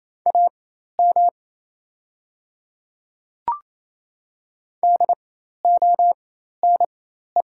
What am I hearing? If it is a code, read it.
Morse code, heard as a keyed tone, about 28 wpm, spaced as for 13 wpm, 700 Hz. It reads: AM DONE
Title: 4x Character Spacing - Top 500 Words - 28wpm